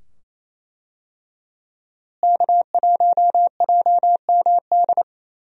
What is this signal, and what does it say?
Morse code, heard as a keyed tone, about 27 wpm, 700 Hz. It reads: K1JMD
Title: USA Call Signs - 5 Characters - Repeated 2x - 28wpm